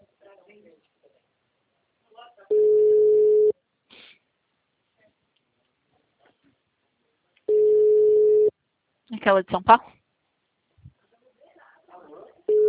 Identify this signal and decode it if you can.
Ringback tone (Europe, 425 Hz), with speech.
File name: out- -2018- - - .29